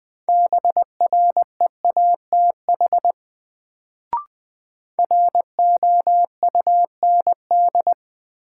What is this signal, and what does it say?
Morse code, heard as a keyed tone, about 21 wpm, 700 Hz. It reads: BREATH ROUND
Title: Sets of 5 Words Encore - New Word Drill - 20wpm